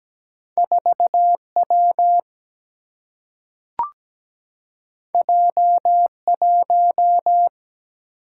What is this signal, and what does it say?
Morse code, heard as a keyed tone, about 17 wpm, 700 Hz. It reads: 4W J1
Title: Single-Letter Single-Number 17wpm